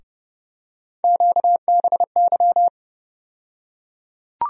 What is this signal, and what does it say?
Morse code, heard as a keyed tone, about 30 wpm, 700 Hz. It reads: QBY
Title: USA Call Sign Postfixes - 3 Letters 30wpm